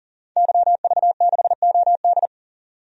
Morse code, heard as a keyed tone, about 37 wpm, 700 Hz. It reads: YV6OD
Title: DXCC-100 Call Signs - 5 Characters - 40wpm